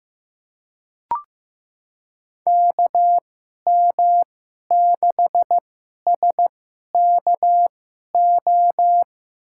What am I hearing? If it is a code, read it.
Morse code, heard as a keyed tone, about 15 wpm, spaced as for 11 wpm, 700 Hz. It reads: KM6SKO
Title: USA Call Signs - 6 Characters - 2x Character Spacing - 15wpm